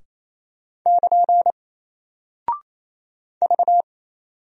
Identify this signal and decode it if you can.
Morse code, heard as a keyed tone, about 28 wpm, 700 Hz. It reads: <KN> V
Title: Single Character (Letter:Number:Punctuation:Prosign) Alt 28wpm